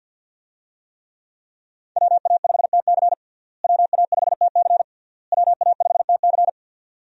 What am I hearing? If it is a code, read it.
Morse code, heard as a keyed tone, about 52 wpm, 700 Hz. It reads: WA5TC WA5TC WA5TC
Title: USA Call Signs - 5 Characters - Repeated 3x - 50wpm